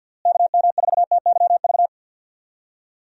Morse code, heard as a keyed tone, about 52 wpm, 700 Hz. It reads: KM4TYV
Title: USA Call Signs - 6 Characters - Repeated 3x - 50wpm